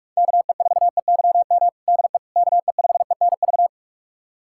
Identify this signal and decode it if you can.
Morse code, heard as a keyed tone, about 41 wpm, 700 Hz. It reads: KE4EYM DE KE5ENV
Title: USA Callsign DE Callsign - 6 Characters - 45wpm